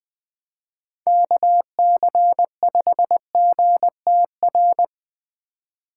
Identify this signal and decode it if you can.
Morse code, heard as a keyed tone, about 25 wpm, 700 Hz. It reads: KC5GTR
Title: Speed Racing - USA 2x3 Call Signs 20wpm